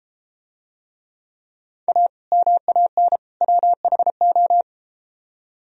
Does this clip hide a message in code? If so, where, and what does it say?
Yes: Morse code from 1.9 to 4.6 s, heard as a keyed tone, about 33 wpm, 700 Hz. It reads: A MAN WHO